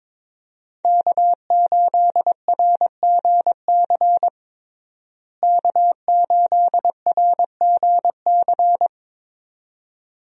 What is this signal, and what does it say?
Morse code, heard as a keyed tone, about 21 wpm, 700 Hz. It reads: K8RGC K8RGC
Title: USA Call Signs - 5 Characters - Repeated 2x - 22wpm